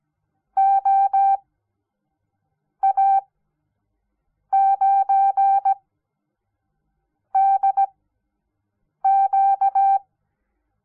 Morse code, heard as a keyed tone, about 17 wpm, spaced as for 5 wpm, 780 Hz. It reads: OA9DQ
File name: Onlinelehrgang / morsen / DJ4UF Morsen06